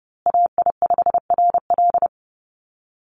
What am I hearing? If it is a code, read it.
Morse code, heard as a keyed tone, about 30 wpm, 700 Hz. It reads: AI5RL
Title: USA Call Signs Repeated 2x 30wpm